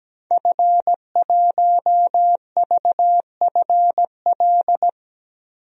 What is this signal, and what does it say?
Morse code, heard as a keyed tone, about 17 wpm, 700 Hz. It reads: F1VFL
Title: DXCC-100 Call Signs - 5 Characters - Repeated 3x - 17wpm